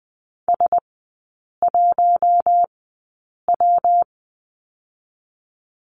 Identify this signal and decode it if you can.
Morse code, heard as a keyed tone, about 20 wpm, spaced as for 8 wpm, 700 Hz. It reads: S1W
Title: 114 Lesson 19e - Letter w - Three Characters 20wpm